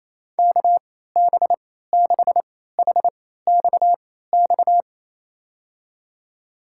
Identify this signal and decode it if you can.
Morse code, heard as a keyed tone, about 27 wpm, spaced as for 16 wpm, 700 Hz. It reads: KB6HXX
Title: USA Call Signs - 6 Characters - 3x Character Spacing - 28wpm